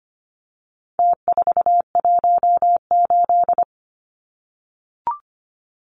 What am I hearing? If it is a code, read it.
Morse code, heard as a keyed tone, about 25 wpm, 700 Hz. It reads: T418
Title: Numbers - Contest Style - 4-Digit 25wpm